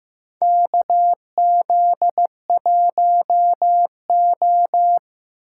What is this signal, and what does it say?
Morse code, heard as a keyed tone, about 15 wpm, 700 Hz. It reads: KZ1O
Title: USA Call Signs - 4 Characters - 4x Character Spacing - 15wpm